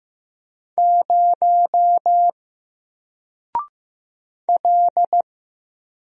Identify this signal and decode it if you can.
Morse code, heard as a keyed tone, about 15 wpm, 700 Hz. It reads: 0 L
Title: Single Letter-Number 15wpm